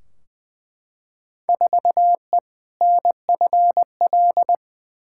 Morse code, heard as a keyed tone, about 20 wpm, 700 Hz. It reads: 4E NFL
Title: Field Day Exchanges Encore - 20wpm